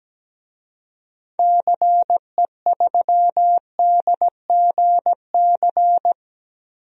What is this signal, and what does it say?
Morse code, heard as a keyed tone, about 17 wpm, 700 Hz. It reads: CE3DGC
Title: DXCC-100 Call Signs - 6 Characters - Repeated 2x - 17wpm